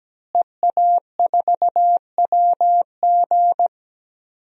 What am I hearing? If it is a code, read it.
Morse code, heard as a keyed tone, about 17 wpm, 700 Hz. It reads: EA4WG
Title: DXCC-100 Call Signs - 5 Characters - Repeated 3x - 17wpm